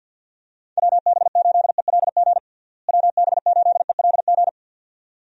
Morse code, heard as a keyed tone, about 52 wpm, 700 Hz. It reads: WB8ELG WB8ELG
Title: USA Call Signs - 6 Characters - Repeated 2x - 50wpm